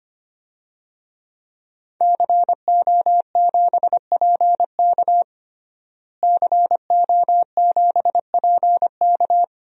Morse code, heard as a keyed tone, about 25 wpm, 700 Hz. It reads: CO7PK CO7PK
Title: DXCC-100 Call Signs - 5 Characters - Repeated 2x - 25wpm